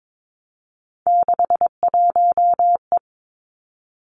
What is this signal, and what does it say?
Morse code, heard as a keyed tone, about 22 wpm, 700 Hz. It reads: 61E
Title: Three Challenging Letter-Number Combinations 22wpm